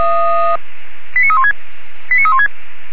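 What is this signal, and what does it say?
Five-tone selective call (ZVEI), heard as quick runs of five tones, tones of about 70 ms. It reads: 89316 89316
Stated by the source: ZVEI five-tone selective call 89330, received by SDR.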